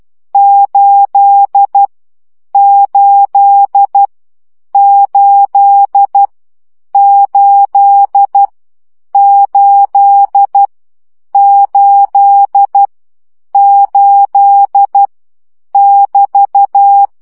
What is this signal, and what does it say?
Morse code, heard as a keyed tone, about 12 wpm, 800 Hz. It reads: 8 8 8 8 8 8 8 <BT>
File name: CW gyakorlas 8 10 baud